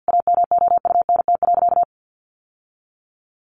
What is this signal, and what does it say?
Morse code, heard as a keyed tone, about 50 wpm, 700 Hz. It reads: AMOUNT.
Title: Sentences from Top 700 Words 50wpm